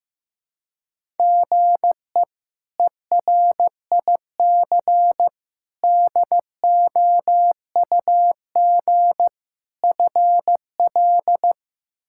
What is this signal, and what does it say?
Morse code, heard as a keyed tone, about 15 wpm, 700 Hz. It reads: GE ERIC DOUG FL
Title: K1USN SST QSO Practice - 15wpm